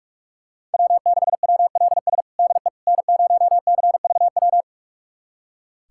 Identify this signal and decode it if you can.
Morse code, heard as a keyed tone, about 41 wpm, 700 Hz. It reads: W6WLS DE N0CVW
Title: USA Callsign DE Callsign - 5 Characters - 45wpm